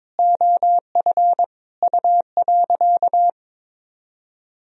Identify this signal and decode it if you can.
Morse code, heard as a keyed tone, about 22 wpm, 700 Hz. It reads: OF U.
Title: Sentences from Top 200 Words and Abbreviations 22wpm